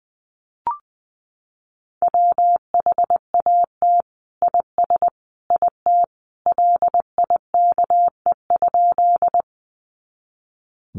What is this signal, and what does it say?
Morse code, heard as a keyed tone, about 20 wpm, 700 Hz. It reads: WHAT IS IT LIKE?
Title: Sentences from Top 100 Words 20wpm